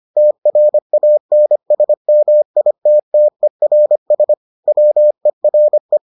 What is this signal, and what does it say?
Morse code, heard as a keyed tone, about 25 wpm, 600 Hz. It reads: TRANSMITTERS WERE